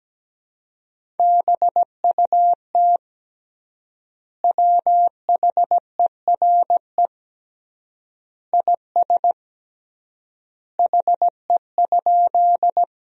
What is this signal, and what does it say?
Morse code, heard as a keyed tone, about 17 wpm, 700 Hz. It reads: BUT WHERE IS HE?
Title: 3x Word-Spacing - Sentences from Top 200 Words - 17wpm